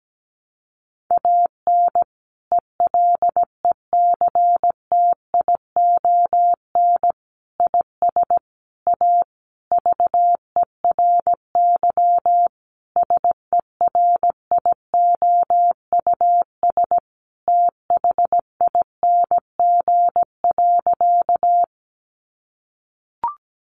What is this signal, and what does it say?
Morse code, heard as a keyed tone, about 17 wpm, 700 Hz. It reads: AN ELECTION IS A VERY SERIOUS THING.
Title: Sentences from Top 1500 Words 17wpm